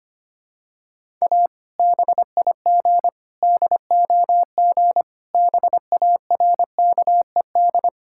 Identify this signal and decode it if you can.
Morse code, heard as a keyed tone, about 25 wpm, 700 Hz. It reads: A BIG DOG BARKED
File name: practice / 25wpm / Sentences with 2-Word Phrases - 25wpm